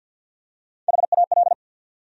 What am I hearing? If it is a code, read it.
Morse code, heard as a keyed tone, about 47 wpm, 700 Hz. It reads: HAL
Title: USA Common Names - Max 3 Letters - 50wpm